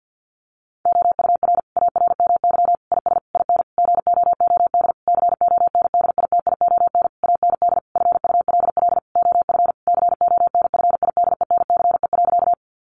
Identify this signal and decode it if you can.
Morse code, heard as a keyed tone, about 50 wpm, 700 Hz. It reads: OUR ARMY IS IN GOOD CONDITION AND FULL OF CONFIDENCE.
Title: Sentences from Top 1500 Words 50wpm